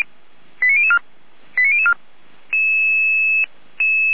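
Five-tone selective call (ZVEI), heard as quick runs of five tones, tones of about 80 ms. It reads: 89004 89004
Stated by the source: ZVEI five-tone call 89508 received on an SDR, recorded by feuerwehr-orlamuende (fire brigade Orlamünde).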